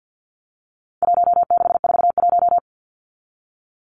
Morse code, heard as a keyed tone, about 50 wpm, 700 Hz. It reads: 1641
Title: Numbers - Contest Style - 4-Digit 50wpm